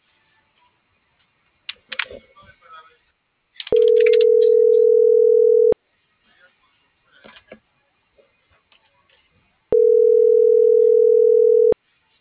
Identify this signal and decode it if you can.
Ringback tone (North America), 440 + 480 Hz.